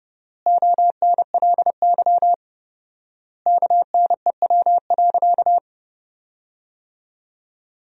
Morse code, heard as a keyed tone, about 30 wpm, 700 Hz. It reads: ONLY KNEW.
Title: 4x Word-Spacing - Sentences from Top 500 Words - 30wpm